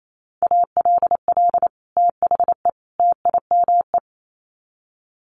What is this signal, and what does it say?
Morse code, heard as a keyed tone, about 28 wpm, 700 Hz. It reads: ALL THE TIME
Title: Sets of 3 Words 28wpm